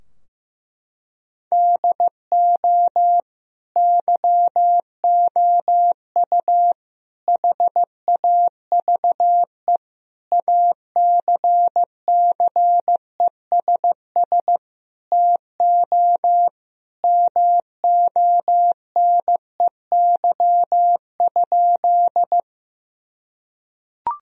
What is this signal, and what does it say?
Morse code, heard as a keyed tone, about 16 wpm, 700 Hz. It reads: DO YOU HAVE ACCESS TO MONEY?
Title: Sentences from Top 1000 Words - Review Part 4 - 15wpm